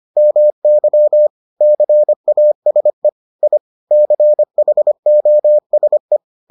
Morse code, heard as a keyed tone, about 25 wpm, 600 Hz. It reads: MY CASE I CHOSE